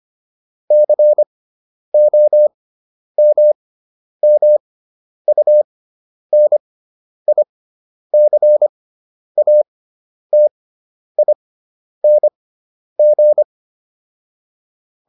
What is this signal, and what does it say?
Morse code, heard as a keyed tone, about 25 wpm, spaced as for 10 wpm, 600 Hz. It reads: COMMUNICATING